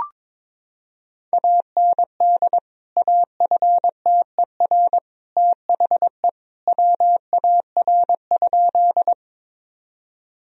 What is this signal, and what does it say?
Morse code, heard as a keyed tone, about 22 wpm, 700 Hz. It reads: AND AFTER THE WAR?
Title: Sentences from Top 1000 Words - Review Part 2 - 22wpm